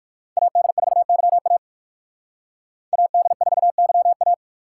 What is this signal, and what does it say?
Morse code, heard as a keyed tone, about 56 wpm, 700 Hz. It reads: AD4YA AD4YA